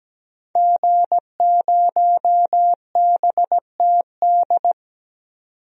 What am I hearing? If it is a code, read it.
Morse code, heard as a keyed tone, about 17 wpm, 700 Hz. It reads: G0BTD
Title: DXCC-100 Call Signs - 5 Characters - 17wpm